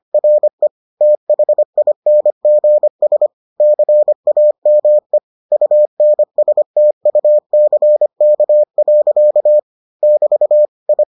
Morse code, heard as a keyed tone, about 25 wpm, 600 Hz. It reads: RE THINGS CAME UNSTUCK. <BT> I